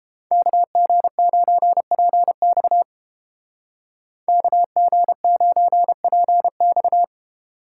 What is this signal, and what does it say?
Morse code, heard as a keyed tone, about 30 wpm, 700 Hz. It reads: KG9PX KG9PX